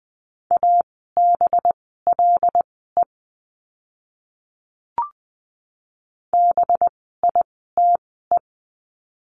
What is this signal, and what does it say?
Morse code, heard as a keyed tone, about 20 wpm, spaced as for 14 wpm, 700 Hz. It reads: ABLE BITE
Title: 185 Lesson 30f - Letter b - Words 20wpm